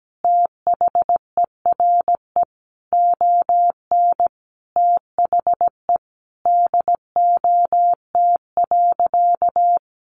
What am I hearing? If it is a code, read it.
Morse code, heard as a keyed tone, about 17 wpm, 700 Hz. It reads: THERE ON THE DOT.
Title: Sentences from Top 1500 Words 17wpm